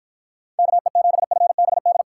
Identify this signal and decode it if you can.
Morse code, heard as a keyed tone, about 52 wpm, 700 Hz. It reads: KE7FBD